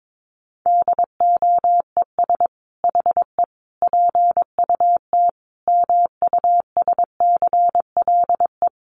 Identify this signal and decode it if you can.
Morse code, heard as a keyed tone, about 22 wpm, 700 Hz. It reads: DOES HE PUT MUSCLE
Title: Sentences from Top 1300 Words 22wpm